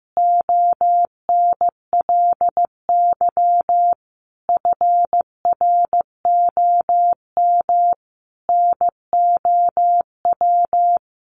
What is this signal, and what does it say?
Morse code, heard as a keyed tone, about 15 wpm, 700 Hz. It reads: ONLY FROM NOW
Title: Easy Phrases 15wpm